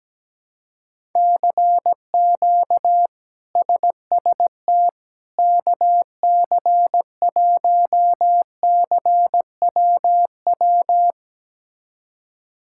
Morse code, heard as a keyed tone, about 17 wpm, 700 Hz. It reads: CQ SST KC1CWW
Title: K1USN SST QSO Practice - 17wpm